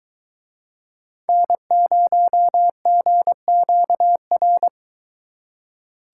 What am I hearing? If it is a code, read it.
Morse code, heard as a keyed tone, about 22 wpm, 700 Hz. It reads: N0GQR